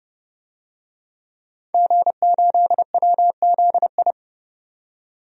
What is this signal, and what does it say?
Morse code, heard as a keyed tone, about 30 wpm, 700 Hz. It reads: G8WZI
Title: DXCC-100 Call Signs - 5 Characters - 30wpm